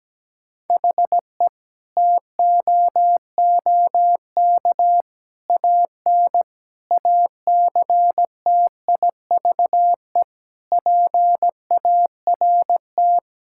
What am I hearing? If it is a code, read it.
Morse code, heard as a keyed tone, about 17 wpm, 700 Hz. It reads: HE TOOK AN ACTIVE PART
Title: Sentences from Top 1000 Words - Review Part 6 - 17wpm